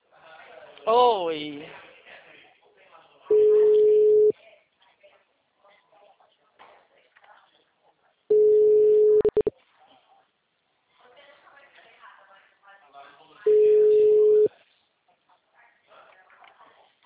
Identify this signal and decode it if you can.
Ringback tone (Europe, 425 Hz), with speech.